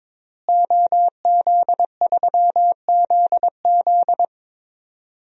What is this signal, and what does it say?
Morse code, heard as a keyed tone, about 22 wpm, 700 Hz. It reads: OZ3ZZ
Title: DXCC-100 Call Signs - 5 Characters - Repeated 3x - 22wpm